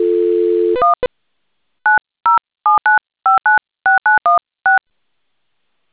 DTMF, heard as telephone touch tones, tones of about 140 ms. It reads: 1 9079596916